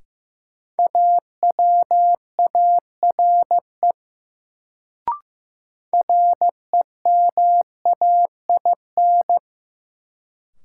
Morse code, heard as a keyed tone, about 15 wpm, 700 Hz. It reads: AWARE REMAIN
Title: Sentences from Top 1200 Words - New Word Drill Alt - 15wpm